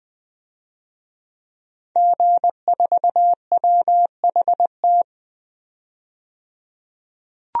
Morse code, heard as a keyed tone, about 20 wpm, 700 Hz. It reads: G4WHT
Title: DXCC-100 Call Signs - 5 Characters - 20wpm